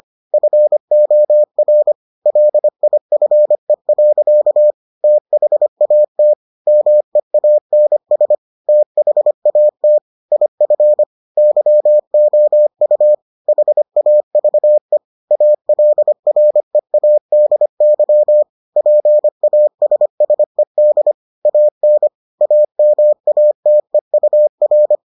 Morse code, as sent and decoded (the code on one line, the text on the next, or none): ..-. --- .-. / .-.. .. ..-. .-.-.- / - .... .- - / -- . .- -. ... / - .... .- - / .. ..-. / -.-- --- ..- / .... .- ...- . / .- .-.. .-. .- -.. -.-- / .--. .- ... ... -.. / .- -. / .- -- .- - . ..- .-.
FOR LIFE. THAT MEANS THAT IF YOU HAVE ALREADY PASSED AN AMATEUR